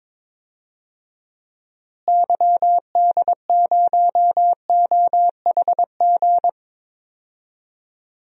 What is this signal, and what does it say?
Morse code, heard as a keyed tone, about 22 wpm, 700 Hz. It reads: YD0OHG
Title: DXCC-100 Call Signs - 6 Characters - Repeated 2x - 22wpm